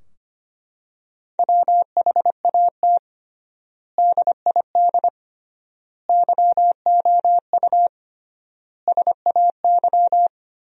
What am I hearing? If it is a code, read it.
Morse code, heard as a keyed tone, about 25 wpm, 700 Hz. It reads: WHAT DID YOU SAY
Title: 3x Word-Spacing - Sentences from Top 200 Words - 25wpm